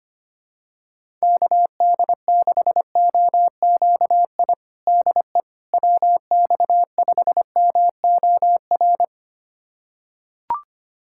Morse code, heard as a keyed tone, about 25 wpm, 700 Hz. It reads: KD6OQI DE WX5MOR
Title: USA Callsign DE Callsign - 6 Characters - 25wpm